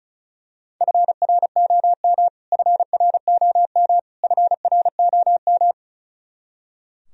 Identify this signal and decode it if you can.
Morse code, heard as a keyed tone, about 36 wpm, 700 Hz. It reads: FROM FROM FROM
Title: Top 100 Words Repeated 35wpm